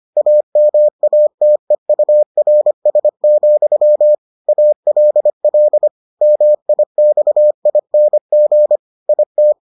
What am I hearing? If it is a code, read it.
Morse code, heard as a keyed tone, about 25 wpm, 600 Hz. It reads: AMATEURS, ALL MIXING IT